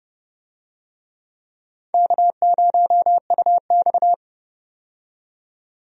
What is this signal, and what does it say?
Morse code, heard as a keyed tone, about 30 wpm, 700 Hz. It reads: K0UX
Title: USA Call Signs - 4 Characters - 30wpm